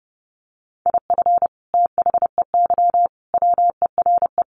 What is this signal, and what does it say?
Morse code, heard as a keyed tone, about 30 wpm, 700 Hz. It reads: IF THEY WERE
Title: Sentences from Top 1200 Words 30wpm